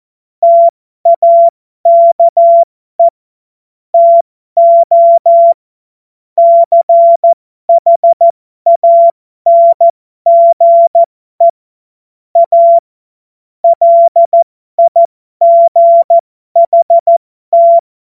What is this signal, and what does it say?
Morse code, heard as a keyed tone, about 14 wpm, spaced as for 12 wpm, 680 Hz. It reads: TAKE TO CHANGE A LIGHT
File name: Morsekurs / Witze